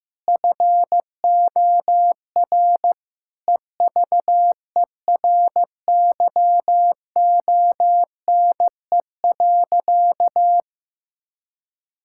Morse code, heard as a keyed tone, about 15 wpm, 700 Hz. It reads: FOR EVERYONE.
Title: Sentences from Top 1000 Words - Review Part 5 - 15wpm